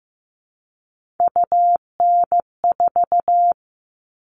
Morse code, heard as a keyed tone, about 15 wpm, 700 Hz. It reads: UN4
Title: All Cut Numbers - 3-Digit 15wpm